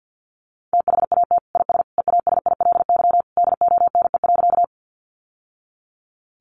Morse code, heard as a keyed tone, about 49 wpm, 700 Hz. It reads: THAT IS EASILY DONE.
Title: Sentences from Top 900 Words 50wpm